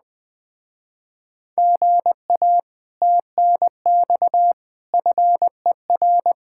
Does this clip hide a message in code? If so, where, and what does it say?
Yes: Morse code from 1.6 to 6.3 s, heard as a keyed tone, about 20 wpm, 700 Hz. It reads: GA TNX FER